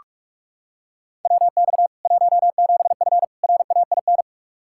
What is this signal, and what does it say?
Morse code, heard as a keyed tone, about 46 wpm, 700 Hz. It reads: WX 17F RAIN
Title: K1USN SST QSO Practice 45wpm